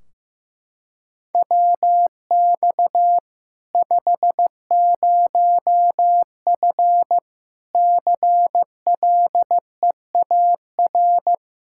Morse code, heard as a keyed tone, about 16 wpm, 700 Hz. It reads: WX 50F CLEAR
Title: K1USN SST QSO Practice 15wpm